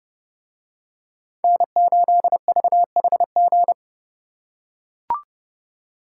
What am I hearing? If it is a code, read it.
Morse code, heard as a keyed tone, about 29 wpm, 700 Hz. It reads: N8VHG